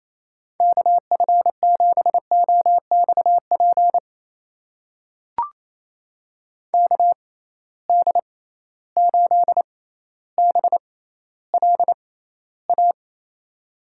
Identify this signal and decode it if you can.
Morse code, heard as a keyed tone, about 28 wpm, spaced as for 14 wpm, 700 Hz. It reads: KF7OXP KD8BLA
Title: USA Call Signs - 6 Characters - 6x Character Spacing - 28wpm